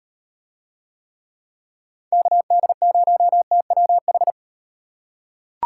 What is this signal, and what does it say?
Morse code, heard as a keyed tone, about 38 wpm, 700 Hz. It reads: KD0TWH